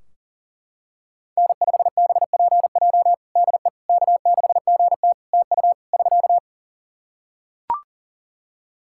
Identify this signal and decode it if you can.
Morse code, heard as a keyed tone, about 37 wpm, 700 Hz. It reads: N5BPJ DE K6GT TU <SK>
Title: K1USN SST QSO Practice 40wpm